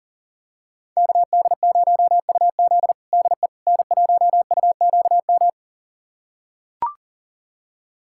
Morse code, heard as a keyed tone, about 40 wpm, 700 Hz. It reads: KD0UZ DE N1UQM
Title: USA Callsign DE Callsign - 5 Characters - 40wpm